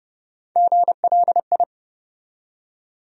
Morse code, heard as a keyed tone, about 30 wpm, 700 Hz. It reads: GLI
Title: USA Call Sign Postfixes - 3 Letters 30wpm